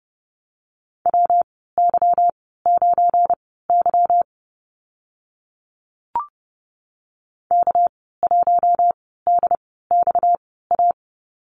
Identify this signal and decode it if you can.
Morse code, heard as a keyed tone, about 30 wpm, spaced as for 17 wpm, 700 Hz. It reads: WY9Y K1DXA
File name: Lesson 37g - Letter x - Call Signs 30wpm